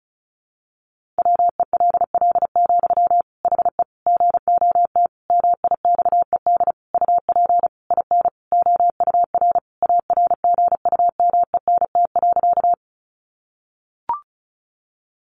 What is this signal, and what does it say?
Morse code, heard as a keyed tone, about 34 wpm, 700 Hz. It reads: WELL, HE GOT MIXED UP IN OUR ARGUMENT.
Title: Sentences from Top 1500 Words 35wpm